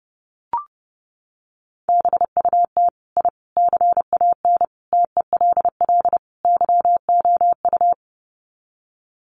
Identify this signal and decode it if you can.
Morse code, heard as a keyed tone, about 30 wpm, 700 Hz. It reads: BUT I CAN TELL YOU
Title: Sets of 5 Words Encore 30wpm